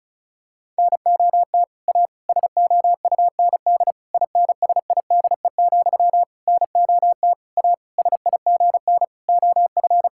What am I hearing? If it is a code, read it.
Morse code, heard as a keyed tone, about 36 wpm, 700 Hz. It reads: NOT A SOUND INSIDE, NOT A SIGN OF
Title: Sentences from Top 1000 Words - Review Part 6 - 35wpm